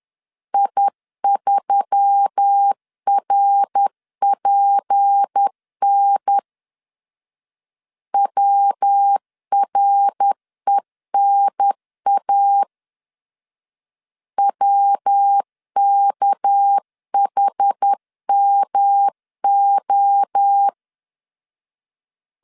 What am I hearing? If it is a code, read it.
Morse code, heard as a keyed tone, about 10 wpm, 800 Hz. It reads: I3RPN WRENA WKHMO
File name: LEZIONI / 6 PKX0 / 6 PK 10wpm 2436